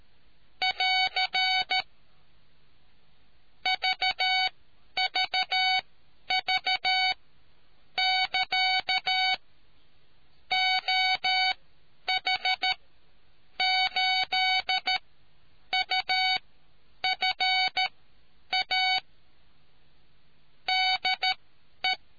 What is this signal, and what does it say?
Morse code, heard as a keyed tone, about 13 wpm, spaced as for 9 wpm, 750 Hz. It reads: <AR> VVV <KA> OH8UFA DE